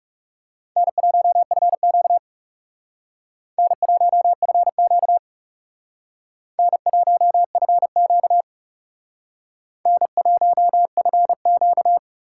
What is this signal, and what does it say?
Morse code, heard as a keyed tone, about 38 wpm, 700 Hz. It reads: N1FQ N1FQ N1FQ N1FQ